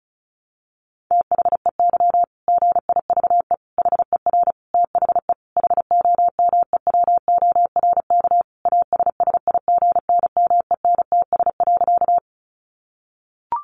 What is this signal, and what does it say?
Morse code, heard as a keyed tone, about 34 wpm, 700 Hz. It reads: THEY GIVE HER THE HOMEWORK ASSIGNMENTS.
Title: Easy Phrases 35wpm